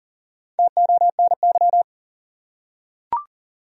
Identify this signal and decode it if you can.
Morse code, heard as a keyed tone, about 37 wpm, 700 Hz. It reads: TONY
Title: USA Common Names - Max 4 Letters - 40wpm